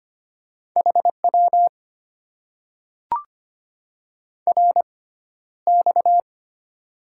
Morse code, heard as a keyed tone, about 25 wpm, spaced as for 13 wpm, 700 Hz. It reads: HW RX
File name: Top 90 QSO Elements - 6x Character Spacing - 25wpm